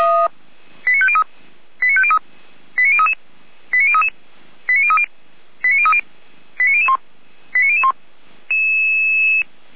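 Five-tone selective call (ZVEI), heard as quick runs of five tones, tones of about 70 ms. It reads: 89503 89503 89033 89033 89030 89030 89001 89001
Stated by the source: ZVEI five-tone call for address 89010, received by SDR, logged by the fire brigade of Orlamünde (feuerwehr-orlamuende).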